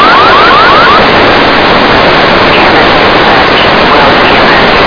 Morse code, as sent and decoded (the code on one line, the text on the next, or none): none